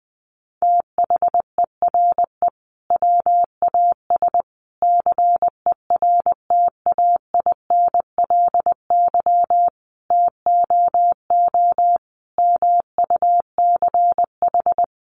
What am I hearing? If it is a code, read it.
Morse code, heard as a keyed tone, about 20 wpm, 700 Hz. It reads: THERE WAS CERTAINLY TOO MUCH
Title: Sentences from Top 900 Words 20wpm